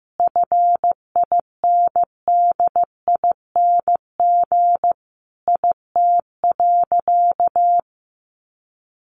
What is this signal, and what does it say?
Morse code, heard as a keyed tone, about 15 wpm, 700 Hz. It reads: FINDING IT.
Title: Sentences from Top 1100 Words 15wpm